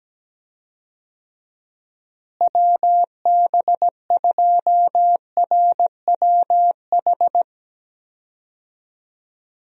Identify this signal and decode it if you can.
Morse code, heard as a keyed tone, about 17 wpm, 700 Hz. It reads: WB2RWH
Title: USA Call Signs - 6 Characters - Repeated 3x - 17wpm